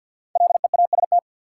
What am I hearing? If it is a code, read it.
Morse code, heard as a keyed tone, about 53 wpm, 700 Hz. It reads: LEAST
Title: Five Letter Words - 50wpm